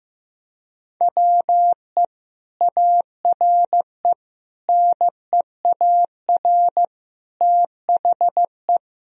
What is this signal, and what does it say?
Morse code, heard as a keyed tone, about 16 wpm, 700 Hz. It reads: WE ARE NEAR THE
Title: Sentences from Top 1000 Words - Review Part 1 - 15wpm